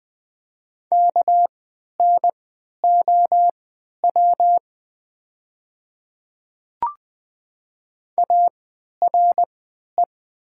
Morse code, heard as a keyed tone, about 20 wpm, spaced as for 11 wpm, 700 Hz. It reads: KNOW ARE
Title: 3x Character Spacing - Top 200 Words - 20wpm